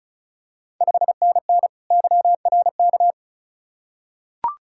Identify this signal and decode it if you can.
Morse code, heard as a keyed tone, about 36 wpm, 690 Hz. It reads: 5NN YRK
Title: Ontario QSO Party Exchange 35wpm